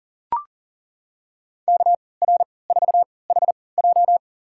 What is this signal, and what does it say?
Morse code, heard as a keyed tone, about 40 wpm, spaced as for 23 wpm, 700 Hz. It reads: KR4HJ
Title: USA Call Signs - 5 Characters - 3x Character Spacing - 40wpm